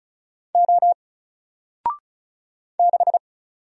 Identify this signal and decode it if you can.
Morse code, heard as a keyed tone, about 36 wpm, 700 Hz. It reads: O 6
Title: Single Character (Letter:Number:Punctuation:Prosign) 35wpm